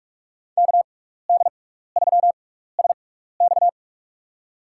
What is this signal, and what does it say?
Morse code, heard as a keyed tone, about 41 wpm, spaced as for 15 wpm, 700 Hz. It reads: KD3SX
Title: USA Call Signs - 5 Characters - 6x Character Spacing - 45wpm